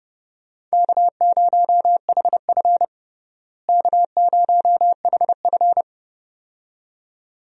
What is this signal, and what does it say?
Morse code, heard as a keyed tone, about 30 wpm, 700 Hz. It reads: K0HF K0HF
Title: USA Call Signs - 4 Characters - Repeated 2x - 30wpm